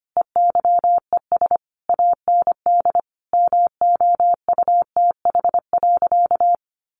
Morse code, heard as a keyed tone, about 25 wpm, 700 Hz. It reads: EYES AND MOUTH.
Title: Easy Phrases 25wpm